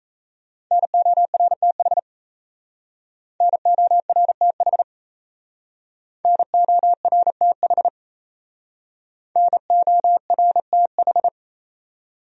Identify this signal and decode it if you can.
Morse code, heard as a keyed tone, about 38 wpm, 700 Hz. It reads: NORTH NORTH NORTH NORTH